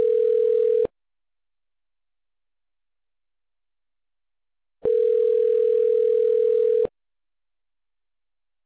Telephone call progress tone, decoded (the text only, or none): ringback tone